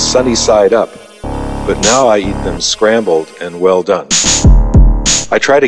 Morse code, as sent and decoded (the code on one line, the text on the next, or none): none